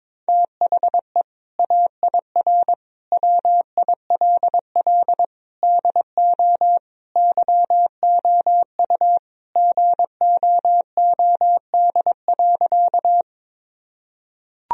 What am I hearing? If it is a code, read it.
Morse code, heard as a keyed tone, about 22 wpm, 700 Hz. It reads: THE AIR WILL DO YOU GOOD.
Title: Sentences from Top 1000 Words - Review Part 1 - 22wpm